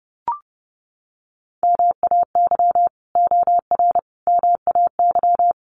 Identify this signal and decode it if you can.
Morse code, heard as a keyed tone, about 30 wpm, 700 Hz. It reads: MAY OR MAY
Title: Sets of 5 Words 30wpm